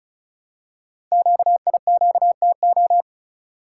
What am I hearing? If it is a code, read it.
Morse code, heard as a keyed tone, about 36 wpm, 690 Hz. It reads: QIQTO
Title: Code Groups - 5 Letters - 35wpm